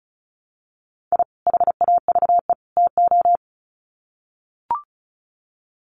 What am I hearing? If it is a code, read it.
Morse code, heard as a keyed tone, about 34 wpm, 700 Hz. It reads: I HAVE TO
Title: Sets of 3 Words 35wpm